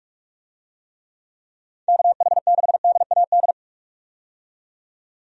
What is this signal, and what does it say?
Morse code, heard as a keyed tone, about 46 wpm, 700 Hz. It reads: KH6DAD
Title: DXCC-100 Call Signs - 6 Characters - 45wpm